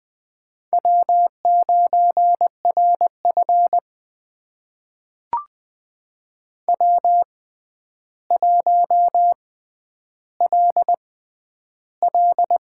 Morse code, heard as a keyed tone, about 21 wpm, spaced as for 10 wpm, 700 Hz. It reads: W9RF W1LL
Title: USA Call Signs - 4 Characters - 6x Character Spacing - 20wpm